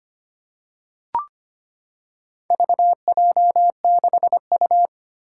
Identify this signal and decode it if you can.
Morse code, heard as a keyed tone, about 25 wpm, 700 Hz. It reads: VJ6U
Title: Four Challenging Letter-Number Combinations 25wpm